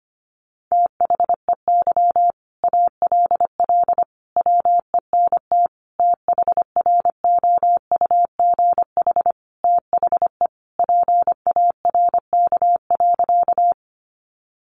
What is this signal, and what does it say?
Morse code, heard as a keyed tone, about 25 wpm, 700 Hz. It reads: THEY ALL WENT THROUGH THE PARK.
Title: Sentences from Top 1300 Words 25wpm